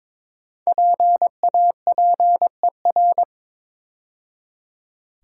Morse code, heard as a keyed tone, about 22 wpm, 700 Hz. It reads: PAPER
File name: 5 Letter Words - First 3 Letters Given - 22wpm